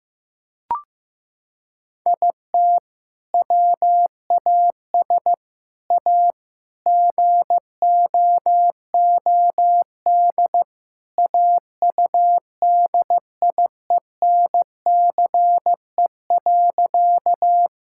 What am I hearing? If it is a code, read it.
Morse code, heard as a keyed tone, about 16 wpm, 700 Hz. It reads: IT WAS A GOOD AUDIENCE.
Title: Sentences from Top 1000 Words - Review Part 5 - 15wpm